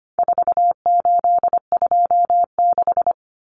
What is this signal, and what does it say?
Morse code, heard as a keyed tone, about 25 wpm, 700 Hz. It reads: 4826
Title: Numbers - 4-Digit 25wpm